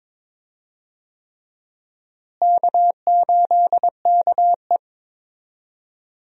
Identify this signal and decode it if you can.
Morse code, heard as a keyed tone, about 22 wpm, 700 Hz. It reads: K8KE
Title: USA Call Signs - 4 Characters - Repeated 3x - 22wpm